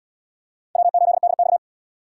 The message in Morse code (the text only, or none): W8DX